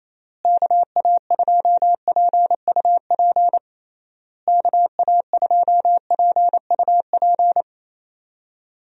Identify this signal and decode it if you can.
Morse code, heard as a keyed tone, about 28 wpm, 700 Hz. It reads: KA2PUP KA2PUP
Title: USA Call Signs - 6 Characters - Repeated 3x - 28wpm